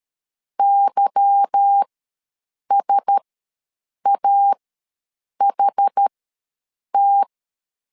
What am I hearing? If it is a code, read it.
Morse code, heard as a keyed tone, about 13 wpm, spaced as for 7 wpm, 800 Hz. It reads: YSAHT
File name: LEZIONI / 10 SIGNS / 10C DOT ws 12wpm 3548